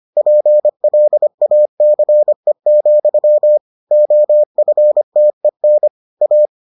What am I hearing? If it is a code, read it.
Morse code, heard as a keyed tone, about 25 wpm, 600 Hz. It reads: PLACE, OFTEN A